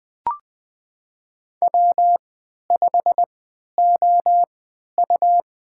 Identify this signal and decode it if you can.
Morse code, heard as a keyed tone, about 20 wpm, spaced as for 11 wpm, 700 Hz. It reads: W5OU
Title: USA Call Signs - 3x Character Spacing - 20wpm